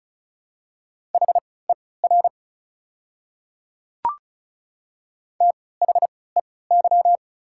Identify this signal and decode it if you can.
Morse code, heard as a keyed tone, about 36 wpm, spaced as for 20 wpm, 700 Hz. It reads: HER THEY
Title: 3x Character Spacing - Top 100 Words - 35wpm